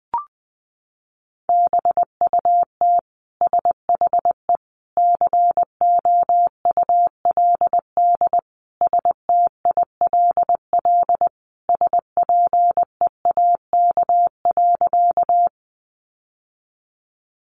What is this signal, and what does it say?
Morse code, heard as a keyed tone, about 20 wpm, 700 Hz. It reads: BUT SHE COULD STILL SPEAK.
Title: Sentences from Top 1100 Words 20wpm